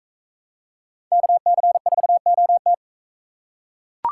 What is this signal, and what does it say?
Morse code, heard as a keyed tone, about 44 wpm, 700 Hz. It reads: KC4OT